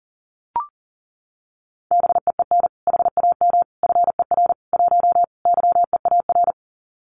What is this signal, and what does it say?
Morse code, heard as a keyed tone, about 40 wpm, 700 Hz. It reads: BEEN HAM FER 1 YEAR
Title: Common QSO Phrases Encore 40wpm